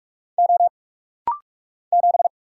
Morse code, heard as a keyed tone, about 44 wpm, 700 Hz. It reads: O 7
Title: Single Letter-Number Alt - Rapid-Fire 45wpm